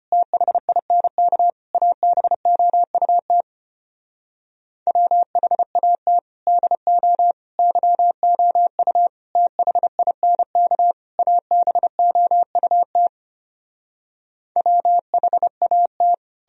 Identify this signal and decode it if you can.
Morse code, heard as a keyed tone, about 33 wpm, 700 Hz. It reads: THINK ABOUT WHAT DO YOU THINK ABOUT WHAT